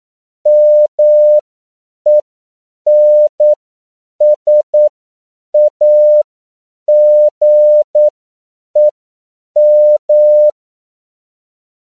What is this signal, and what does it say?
Morse code, heard as a keyed tone, about 9 wpm, spaced as for 7 wpm, 600 Hz. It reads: MENSAGEM
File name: CW / 10PPM